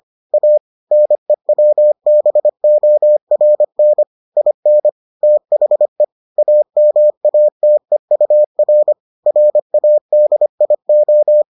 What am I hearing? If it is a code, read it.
Morse code, heard as a keyed tone, about 25 wpm, 600 Hz. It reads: A NEWBORN IN THE AMATEUR RADIO